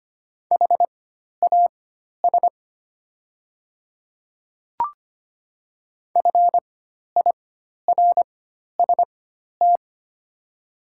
Morse code, heard as a keyed tone, about 25 wpm, spaced as for 12 wpm, 700 Hz. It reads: HAS FIRST
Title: 4x Character Spacing - Top 100 Words - 25wpm